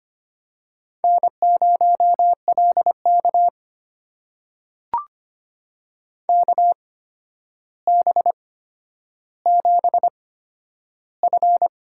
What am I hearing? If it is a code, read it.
Morse code, heard as a keyed tone, about 25 wpm, spaced as for 11 wpm, 700 Hz. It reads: N0LK KB7F
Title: USA Call Signs - 4 Characters - 8x Character Spacing - 25wpm